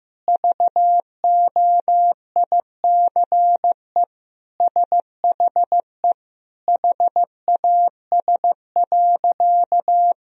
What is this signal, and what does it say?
Morse code, heard as a keyed tone, about 15 wpm, 700 Hz. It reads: VOICE SHE HAS.
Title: Sentences from Top 500 Words and Abbreviations 15wpm